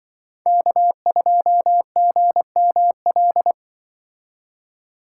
Morse code, heard as a keyed tone, about 23 wpm, 700 Hz. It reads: K2GML
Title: Speed Racing - USA 1x3 Call Signs 20wpm